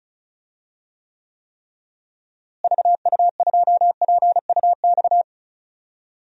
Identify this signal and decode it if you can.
Morse code, heard as a keyed tone, about 36 wpm, 700 Hz. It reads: VU2PUX
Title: DXCC-100 Call Signs - 6 Characters - Repeated 3x - 35wpm